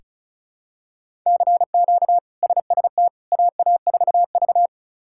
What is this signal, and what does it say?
Morse code, heard as a keyed tone, about 36 wpm, 700 Hz. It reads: CQ SST AA4V
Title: K1USN SST QSO Practice - 35wpm